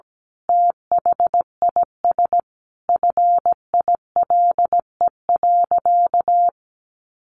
Morse code, heard as a keyed tone, about 17 wpm, 700 Hz. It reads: THIS FILE.
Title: Sentences from Top 1200 Words 17wpm